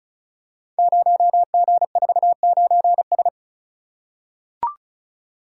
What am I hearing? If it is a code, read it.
Morse code, heard as a keyed tone, about 35 wpm, 690 Hz. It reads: 0G49S
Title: Code Groups - 5 Letter:Numbers - 35wpm